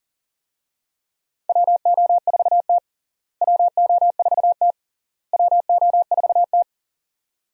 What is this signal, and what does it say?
Morse code, heard as a keyed tone, about 40 wpm, 700 Hz. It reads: WO4T WO4T WO4T
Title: USA Call Signs - 4 Characters - Repeated 3x - 40wpm